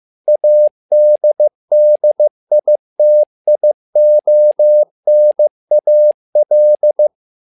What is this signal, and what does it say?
Morse code, heard as a keyed tone, about 15 wpm, 600 Hz. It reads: ADDITIONAL
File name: learning center / tower trouble / tower trouble 15wpm